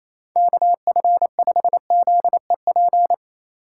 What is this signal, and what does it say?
Morse code, heard as a keyed tone, about 28 wpm, 700 Hz. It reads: KF5ZEP
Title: USA Call Signs - 6 Characters - 2x Character Spacing - 28wpm